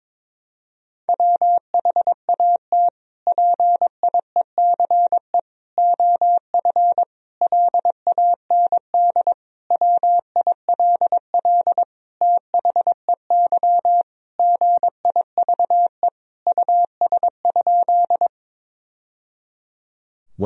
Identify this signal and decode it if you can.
Morse code, heard as a keyed tone, about 22 wpm, 700 Hz. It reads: WHAT PIECE OF LAND WILL THEY GIVE US?
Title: Sentences from Top 1000 Words - Review Part 2 - 22wpm